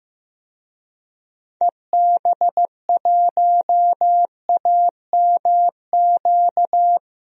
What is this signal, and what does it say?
Morse code, heard as a keyed tone, about 15 wpm, 700 Hz. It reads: EB1AMQ
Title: DXCC-100 Call Signs - 6 Characters - 15wpm